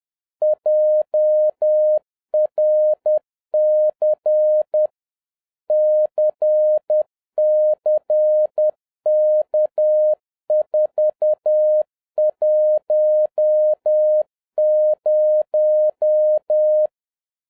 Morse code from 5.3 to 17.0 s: CCK410